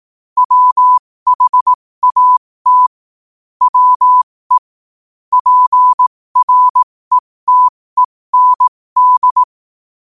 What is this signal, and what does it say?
Morse code, heard as a keyed tone, about 18 wpm, spaced as for 15 wpm, 1000 Hz. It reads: WHAT WE PRETEND
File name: qotdfiles / 15wpm